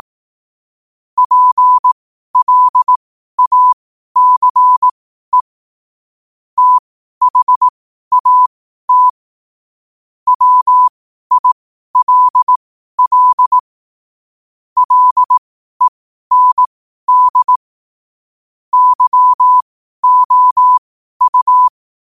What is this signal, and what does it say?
Morse code, heard as a keyed tone, about 18 wpm, spaced as for 12 wpm, 1000 Hz. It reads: PLACE THAT WILL LEND YOU
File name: qotdfiles / 12wpm